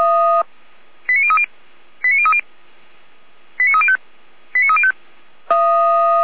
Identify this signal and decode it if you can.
Five-tone selective call (ZVEI), heard as quick runs of five tones, tones of about 70 ms. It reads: 89030 89030 89305 89305